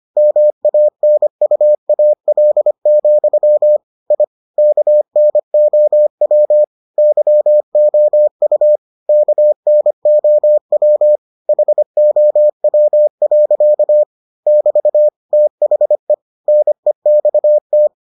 Morse code, heard as a keyed tone, about 25 wpm, 600 Hz. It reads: MANUAL, I KNOW YOU KNOW HOW. <BT> THE NEXT